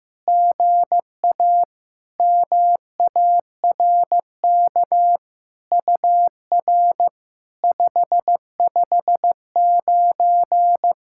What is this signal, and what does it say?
Morse code, heard as a keyed tone, about 15 wpm, 690 Hz. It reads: GA MARK UR 559
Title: SOTA QSO Practice 15wpm